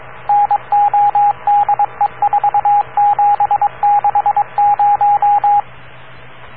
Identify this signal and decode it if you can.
Morse code, heard as a keyed tone, about 22 wpm, 820 Hz, with radio noise. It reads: NODE4760